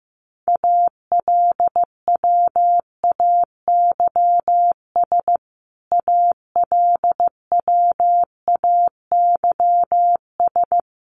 Morse code, heard as a keyed tone, about 15 wpm, 700 Hz. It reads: ALWAYS ALWAYS
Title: Top 300 Words Repeated 15wpm